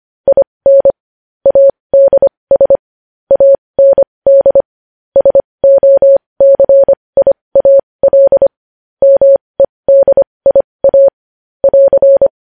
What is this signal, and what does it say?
Morse code, heard as a keyed tone, about 25 wpm, spaced as for 20 wpm, 550 Hz. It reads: IN ADS AND SOCIAL MEDIA <AR>
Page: news-headlines-in-morse-code-at-20-wpm